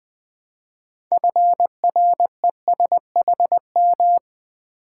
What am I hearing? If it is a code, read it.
Morse code, heard as a keyed tone, about 20 wpm, 700 Hz. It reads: FRESHM
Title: Build A Word - 2 Letters At A Time - 8-Letter Words - 20wpm